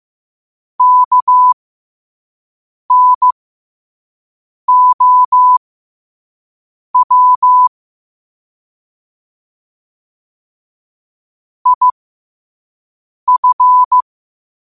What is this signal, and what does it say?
Morse code, heard as a keyed tone, about 15 wpm, spaced as for 5 wpm, 1000 Hz. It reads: KNOW IF
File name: qotdfiles / 5wpm